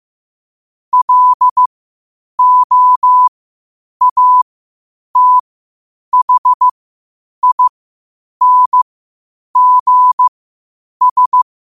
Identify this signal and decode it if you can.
Morse code, heard as a keyed tone, about 15 wpm, spaced as for 8 wpm, 1000 Hz. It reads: LOATHINGS
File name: qotdfiles / 8wpm